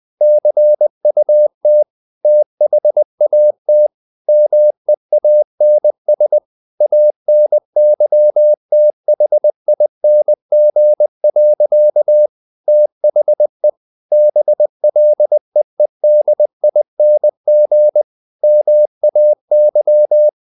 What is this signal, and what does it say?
Morse code, heard as a keyed tone, about 20 wpm, 600 Hz. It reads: CUT THAT MEANS ANYTHING. THE BLEEDING MAY
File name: the old man and the sea part 2 20wpm